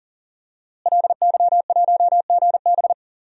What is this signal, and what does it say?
Morse code, heard as a keyed tone, about 40 wpm, 700 Hz. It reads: LY1GB
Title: DXCC-100 Call Signs - 5 Characters - Repeated 3x - 40wpm